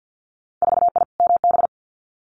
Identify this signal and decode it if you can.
Morse code, heard as a keyed tone, about 50 wpm, 700 Hz. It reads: 4I MB
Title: Winter Field Day Exchanges Encore 50wpm